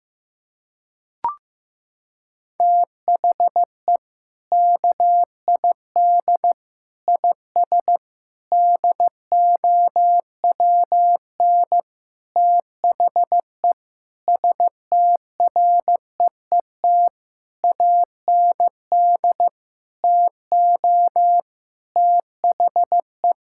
Morse code, heard as a keyed tone, about 16 wpm, 700 Hz. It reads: THE KID IS DOWN THE STREET AND TO THE
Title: Sentences from Top 1600 Words 15wpm